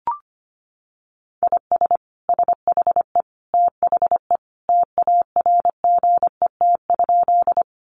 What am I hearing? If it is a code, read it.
Morse code, heard as a keyed tone, about 25 wpm, 700 Hz. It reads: IS SHE THE TARGET?
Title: Sentences from Top 1200 Words 25wpm